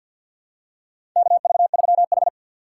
Morse code, heard as a keyed tone, about 52 wpm, 700 Hz. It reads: KV3H
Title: USA Call Signs - 4 Characters - 2x Character Spacing - 50wpm